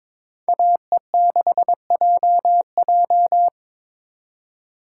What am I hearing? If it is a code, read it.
Morse code, heard as a keyed tone, about 27 wpm, 700 Hz. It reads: AE6JJ
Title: Speed Racing - USA 2x2 Call Signs 22wpm